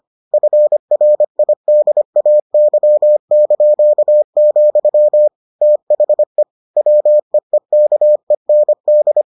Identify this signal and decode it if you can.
Morse code, heard as a keyed tone, about 25 wpm, 600 Hz. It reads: FRIDAY), THE WEEKEND